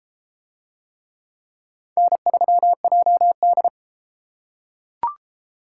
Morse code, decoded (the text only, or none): N3JD